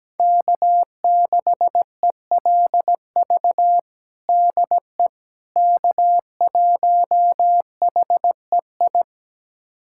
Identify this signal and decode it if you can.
Morse code, heard as a keyed tone, about 17 wpm, 700 Hz. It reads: K6ELV DE K1HEI
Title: USA Callsign DE Callsign - 5 Characters - 17wpm